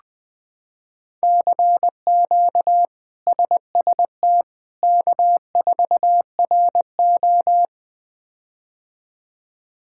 Morse code, heard as a keyed tone, about 20 wpm, 700 Hz. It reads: CQ SST K4RO
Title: K1USN SST QSO Practice - 20wpm